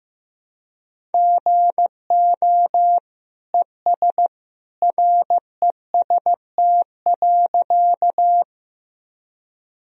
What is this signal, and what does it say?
Morse code, heard as a keyed tone, about 15 wpm, 700 Hz. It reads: GO ES REST.
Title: Sentences from Top 500 Words and Abbreviations 15wpm